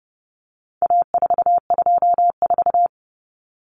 Morse code, heard as a keyed tone, about 30 wpm, 700 Hz. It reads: A424
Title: Common Cut Numbers - 4-Digit 30wpm